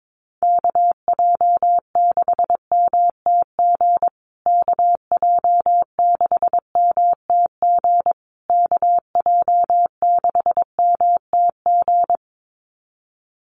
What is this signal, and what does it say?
Morse code, heard as a keyed tone, about 22 wpm, 700 Hz. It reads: KJ6MTG KJ6MTG KJ6MTG
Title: USA Call Signs Repeated 3x 22wpm